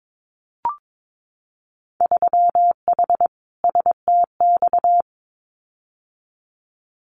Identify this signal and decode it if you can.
Morse code, heard as a keyed tone, about 22 wpm, 700 Hz. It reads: 3H STX
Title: Winter Field Day Exchanges 22wpm